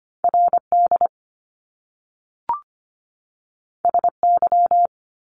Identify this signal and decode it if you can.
Morse code, heard as a keyed tone, about 25 wpm, 700 Hz. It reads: RD SY
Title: Top Two Letter Combinations in Words 25wpm